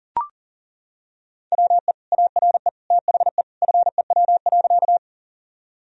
Morse code, heard as a keyed tone, about 40 wpm, 700 Hz. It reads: WE ARE THE FEW.
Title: Sentences from Top 1000 Words - Review Part 1 - 40wpm